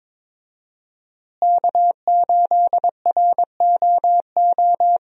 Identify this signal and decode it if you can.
Morse code, heard as a keyed tone, about 22 wpm, 700 Hz. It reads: K8ROO
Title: USA Call Signs - 5 Characters - Repeated 3x - 22wpm